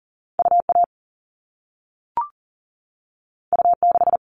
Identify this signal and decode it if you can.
Morse code, heard as a keyed tone, about 40 wpm, 700 Hz. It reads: UA U6
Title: All Cut Numbers - 2-Digit 40wpm